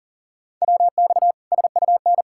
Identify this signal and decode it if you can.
Morse code, heard as a keyed tone, about 40 wpm, 700 Hz. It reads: WX SUN
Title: Practice QSO - Exchange 2 - Info - 40wpm